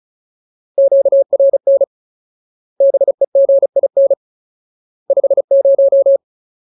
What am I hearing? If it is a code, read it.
Morse code, heard as a keyed tone, about 34 wpm, 550 Hz. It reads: QRN BEGIN 50
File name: Ultimate Word List / 35WPM Ultimate